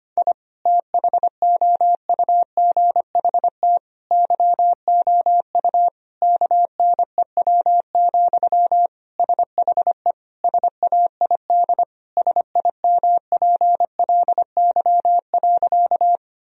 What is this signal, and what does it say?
Morse code, heard as a keyed tone, about 25 wpm, 700 Hz. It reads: I THOUGHT YOU KNEW, SHE SAID SIMPLY.
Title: Sentences from Top 1000 Words - Review Part 4 - 25wpm